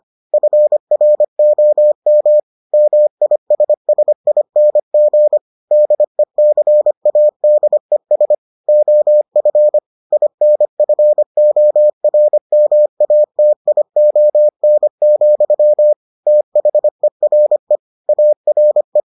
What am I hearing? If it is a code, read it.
Morse code, heard as a keyed tone, about 25 wpm, 600 Hz. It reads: FROM MISSING DECADES OF INFORMATION, THERE ARE